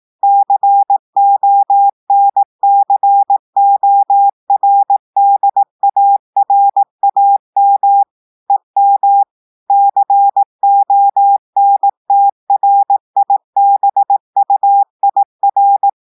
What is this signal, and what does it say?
Morse code, heard as a keyed tone, about 18 wpm, 800 Hz. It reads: CONCORDARAM EM CONTRIBUIR
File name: audios / portugais / portugais texte en clair 1 vitesse 18 mots minutes